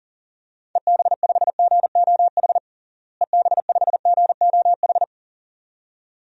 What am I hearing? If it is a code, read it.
Morse code, heard as a keyed tone, about 40 wpm, 700 Hz. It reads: EB5GOH EB5GOH
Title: DXCC-100 Call Signs - 6 Characters - Repeated 2x - 40wpm